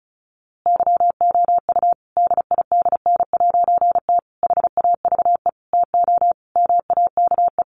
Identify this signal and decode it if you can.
Morse code, heard as a keyed tone, about 34 wpm, 700 Hz. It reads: YOU DIDN'T HAVE TO MAKE
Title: A Sound Of Thunder 35wpm